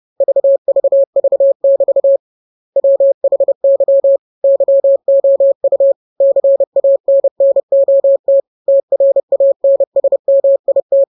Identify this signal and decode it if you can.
Morse code, heard as a keyed tone, about 30 wpm, 550 Hz. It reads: VVV<BT> WHY YOU CANNOT TRANSMIT
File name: jokes / joke 20 30 0000